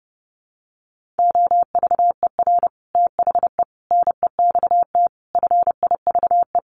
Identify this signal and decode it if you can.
Morse code, heard as a keyed tone, about 30 wpm, 700 Hz. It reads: OVER THE NEXT FIVE
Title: Sets of 5 Words 30wpm